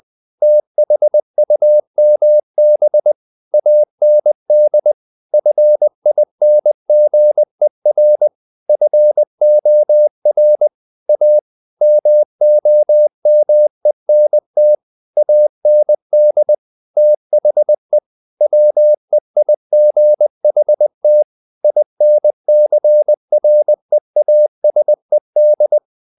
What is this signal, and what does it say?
Morse code, heard as a keyed tone, about 20 wpm, 600 Hz. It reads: THUMB AND FINGER FOR A MOMENT AND THE WEIGHT INCREASED
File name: the old man and the sea / the old man and the sea part 1 20wpm